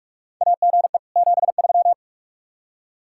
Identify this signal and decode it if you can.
Morse code, heard as a keyed tone, about 46 wpm, 700 Hz. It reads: AGE 73
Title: Practice QSO - Exchange 3 - More Info - 45wpm